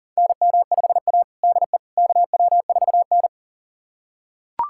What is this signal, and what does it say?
Morse code, heard as a keyed tone, about 40 wpm, 700 Hz. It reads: NM5A DE KW4N
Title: USA Callsign DE Callsign - 4 Characters - 40wpm